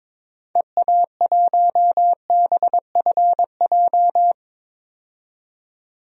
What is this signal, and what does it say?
Morse code, heard as a keyed tone, about 22 wpm, 700 Hz. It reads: EA1BFJ
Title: DXCC-100 Call Signs - 6 Characters - 22wpm